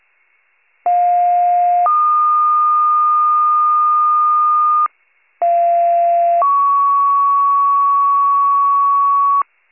Two-tone page, heard as a one-second tone, then a three-second tone, about 710 then 1220 Hz, about 690 then 1120 Hz. It is heard from 0.9 to 9.4 s.